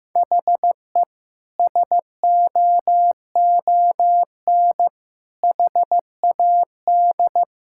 Morse code, heard as a keyed tone, about 16 wpm, 700 Hz. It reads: HE SOON HAD
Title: Sentences from Top 1600 Words 15wpm